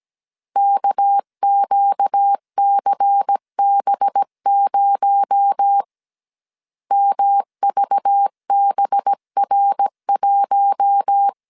Morse code, heard as a keyed tone, about 17 wpm, 800 Hz. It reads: KQCB0 MVBR1
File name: 16wpm